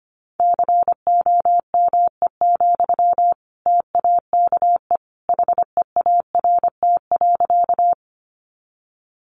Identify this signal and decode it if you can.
Morse code, heard as a keyed tone, about 25 wpm, 700 Hz. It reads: COME, TAKE HEART.
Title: Sentences from Top 1200 Words 25wpm